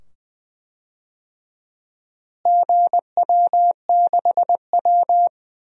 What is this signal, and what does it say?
Morse code, heard as a keyed tone, about 20 wpm, 700 Hz. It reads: GW6W
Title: DXCC-100 Call Signs - 4 Characters - Repeated 2x - 20wpm